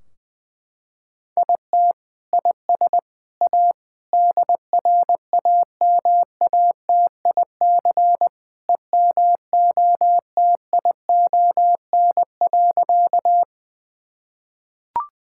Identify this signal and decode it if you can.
Morse code, heard as a keyed tone, about 19 wpm, 700 Hz. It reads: IT IS A DRAMATIC EMOTION.
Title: Sentences from Top 1600 Words 20wpm